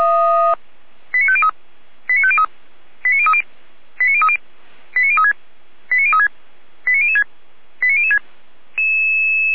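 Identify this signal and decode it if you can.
Five-tone selective call (ZVEI), heard as quick runs of five tones, tones of about 80 ms. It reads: 89503 89503 89030 89030 89036 89036 89006 89006
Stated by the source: ZVEI five-tone selective call received on SDR.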